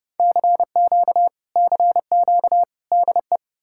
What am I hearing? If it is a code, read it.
Morse code, heard as a keyed tone, about 40 wpm, 700 Hz. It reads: CQ CQ DE